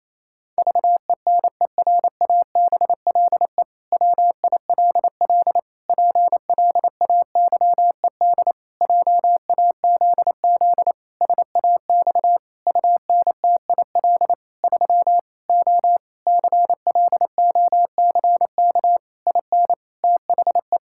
Morse code, heard as a keyed tone, about 28 wpm, 700 Hz. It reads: VENERABLE WILL PLAYED JAZZ SAX UNTIL 3 O CLOCK IN THE
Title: Pangram Challenge v2 - 28wpm